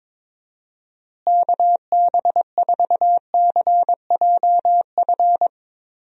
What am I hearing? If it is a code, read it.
Morse code, heard as a keyed tone, about 22 wpm, 700 Hz. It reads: KB4CJF
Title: USA Call Signs - 6 Characters - Repeated 2x - 22wpm